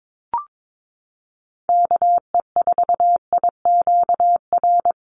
Morse code, heard as a keyed tone, about 22 wpm, 700 Hz. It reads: KE4IQR
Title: USA Call Signs 22wpm